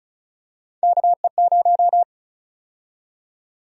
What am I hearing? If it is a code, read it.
Morse code, heard as a keyed tone, about 36 wpm, 700 Hz. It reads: KE0
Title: USA Call Sign Prefixes - Repeated 2x - 35wpm